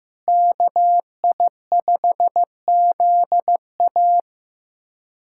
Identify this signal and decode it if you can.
Morse code, heard as a keyed tone, about 15 wpm, 700 Hz. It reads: KI5ZA
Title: USA Callsign DE Callsign - 5 Characters - 15wpm